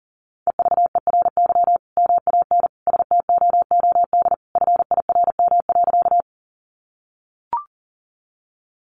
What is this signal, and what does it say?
Morse code, heard as a keyed tone, about 37 wpm, 700 Hz. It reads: EVERY MAN STOOD FIRM.
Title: Sentences from Top 1400 Words 40wpm